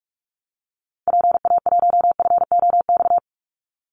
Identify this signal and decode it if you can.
Morse code, heard as a keyed tone, about 44 wpm, 700 Hz. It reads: PA1FOX
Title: Call Signs Repeated 2x - 45wpm